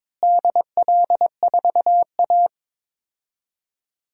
Morse code, heard as a keyed tone, about 21 wpm, 700 Hz. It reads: DL4A
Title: DXCC-100 Call Signs - 4 Characters - Repeated 2x - 22wpm